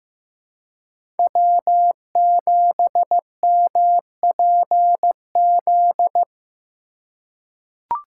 Morse code, heard as a keyed tone, about 15 wpm, 700 Hz. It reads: W7MPZ
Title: USA Call Signs - 5 Characters - 4x Character Spacing - 15wpm